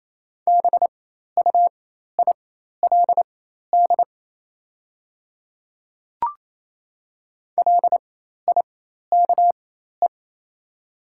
Morse code, heard as a keyed tone, about 28 wpm, spaced as for 13 wpm, 700 Hz. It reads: BUILD LIKE
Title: 4x Character Spacing - Top 200 Words - 28wpm